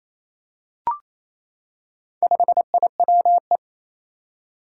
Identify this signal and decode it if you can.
Morse code, heard as a keyed tone, about 28 wpm, 700 Hz. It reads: 5IWE
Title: Four Challenging Letter-Number Combinations 28wpm